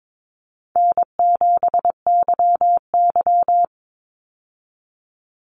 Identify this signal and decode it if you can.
Morse code, heard as a keyed tone, about 22 wpm, 690 Hz. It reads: N7YY
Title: Call Signs Repeated 2x - 22wpm